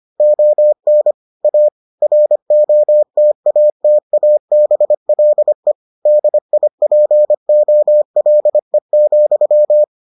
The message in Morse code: ON A ROTATABLE DIPOLE,

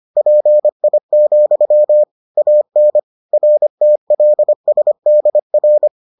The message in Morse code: PI, AN RTLSDR

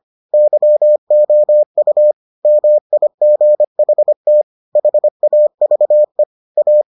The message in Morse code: YOU MIGHT HAVE A

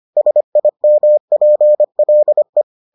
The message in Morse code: SIMPLE

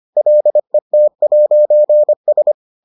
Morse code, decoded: LET'S

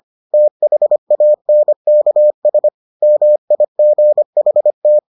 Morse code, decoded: THANKS MIGHT

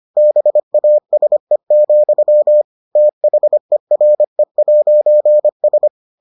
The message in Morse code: BASE, THERE'S

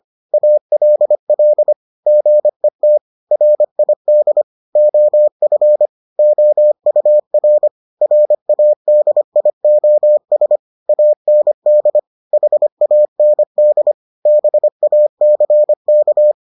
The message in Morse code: ALL GET RID OF OUR RADIOS AND HAND BACK

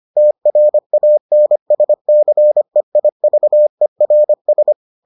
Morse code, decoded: TRANSCEIVERS

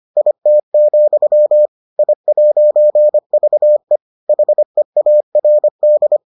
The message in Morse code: IT, I'VE HEARD

